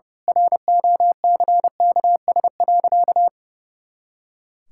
Morse code, heard as a keyed tone, about 30 wpm, 700 Hz. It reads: ROCKS.
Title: Pangram Challenge v2 - 30wpm